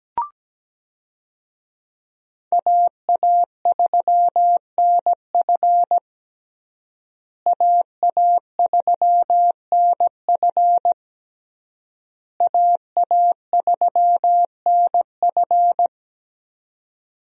Morse code, heard as a keyed tone, about 17 wpm, 700 Hz. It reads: AA3NF AA3NF AA3NF
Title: USA Call Signs - 5 Characters - Repeated 3x - 17wpm